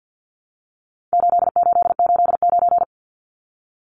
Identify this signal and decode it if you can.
Morse code, heard as a keyed tone, about 50 wpm, 700 Hz. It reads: 8889
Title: Numbers - 4-Digit 50wpm